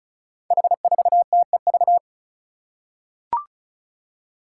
Morse code, heard as a keyed tone, about 36 wpm, 700 Hz. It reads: H4TEV
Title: Five Challenging Letter-Number Combinations 35wpm